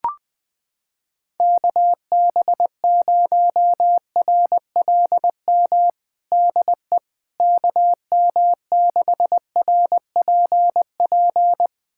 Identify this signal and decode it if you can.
Morse code, heard as a keyed tone, about 21 wpm, 700 Hz. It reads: KB0RLM DE KM6RPP
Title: USA Callsign DE Callsign - 6 Characters - 20wpm